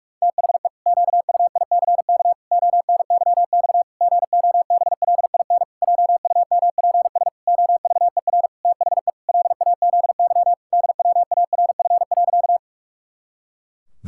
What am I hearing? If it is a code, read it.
Morse code, heard as a keyed tone, about 41 wpm, 700 Hz. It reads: THE QUICK ONYX GOBLIN JUMPS OVER THE LAZY DWARF.
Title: Pangram Challenge v2 - 45wpm